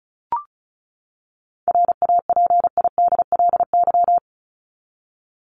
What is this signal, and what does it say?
Morse code, heard as a keyed tone, about 34 wpm, 700 Hz. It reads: RAPIDLY
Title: LY Suffix - Top 100 Words with Base Words 35wpm